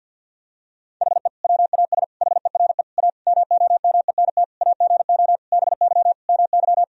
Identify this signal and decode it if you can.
Morse code, heard as a keyed tone, about 49 wpm, 700 Hz. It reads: HE WAS HERE A MOMENT AGO BY MY
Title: Sentences from Top 1000 Words - Review Part 3 - 50wpm